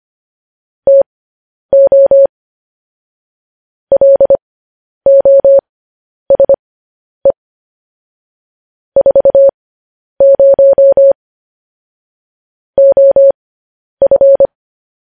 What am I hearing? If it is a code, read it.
Morse code, heard as a keyed tone, about 25 wpm, spaced as for 10 wpm, 550 Hz. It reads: TO LOSE 40 OF
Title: News Headlines at 10